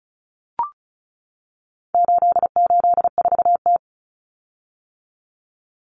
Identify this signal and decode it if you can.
Morse code, heard as a keyed tone, about 34 wpm, 700 Hz. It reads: 884T
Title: Common Cut Numbers - 4-Digit 35wpm